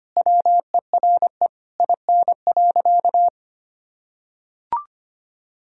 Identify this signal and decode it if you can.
Morse code, heard as a keyed tone, about 25 wpm, 700 Hz. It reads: WERE IN.
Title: Sentences from Top 1000 Words - Review Part 3 - 25wpm